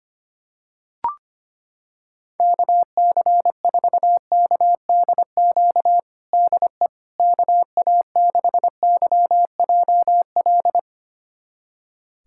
Morse code, heard as a keyed tone, about 25 wpm, 700 Hz. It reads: KC4KDQ DE KA6YJL
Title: USA Callsign DE Callsign - 6 Characters - 25wpm